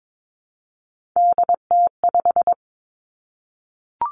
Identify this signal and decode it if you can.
Morse code, heard as a keyed tone, about 22 wpm, 700 Hz. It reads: DT5
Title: Three Challenging Letter-Number Combinations 22wpm